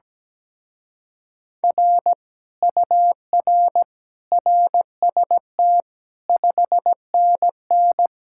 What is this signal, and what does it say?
Morse code, heard as a keyed tone, about 17 wpm, 700 Hz. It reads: R UR RST 5NN
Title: Speed-Racing Common QSO Phrases Encore - 17wpm